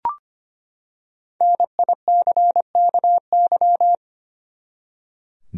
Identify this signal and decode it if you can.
Morse code, heard as a keyed tone, about 25 wpm, 700 Hz. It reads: NICKY
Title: USA Common Names - Max 5 Letters - 25wpm